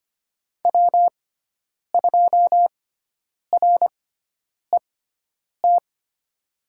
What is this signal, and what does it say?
Morse code, heard as a keyed tone, about 25 wpm, spaced as for 9 wpm, 700 Hz. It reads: W2RET
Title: USA Call Signs - 5 Characters - 6x Character Spacing - 25wpm